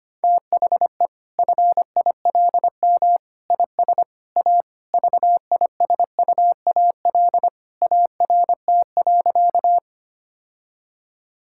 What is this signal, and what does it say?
Morse code, heard as a keyed tone, about 25 wpm, 700 Hz. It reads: THE FILM IS A VISUAL ART.
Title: Sentences from Top 1000 Words - Review Part 6 - 25wpm